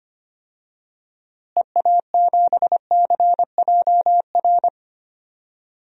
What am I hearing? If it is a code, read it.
Morse code, heard as a keyed tone, about 25 wpm, 700 Hz. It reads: EA7CJR
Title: DXCC-100 Call Signs - 6 Characters - Repeated 3x - 25wpm